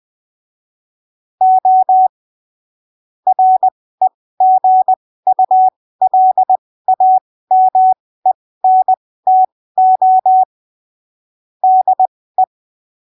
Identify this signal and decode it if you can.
Morse code, heard as a keyed tone, about 20 wpm, spaced as for 15 wpm, 750 Hz. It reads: O REGULAMENTO DE
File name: arquivos / texto / texto 12 20ppm